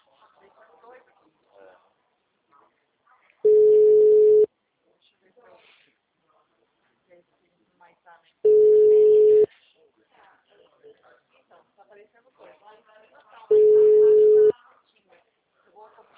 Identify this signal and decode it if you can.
Ringback tone (Europe, 425 Hz), with speech.